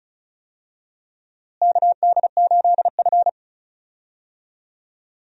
Morse code, heard as a keyed tone, about 36 wpm, 700 Hz. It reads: KD8F
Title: USA Call Signs - 4 Characters - Repeated 2x - 35wpm